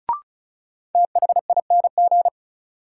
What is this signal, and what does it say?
Morse code, heard as a keyed tone, about 36 wpm, 700 Hz. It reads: THING
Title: Sets of 3 Words - New Word Drill - 35wpm